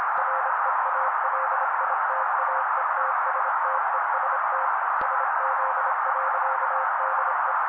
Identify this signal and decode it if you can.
Morse code, heard as a keyed tone, about 27 wpm, 530 Hz. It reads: REALITAETSTEST SZ.DE
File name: daily / 25wpm noise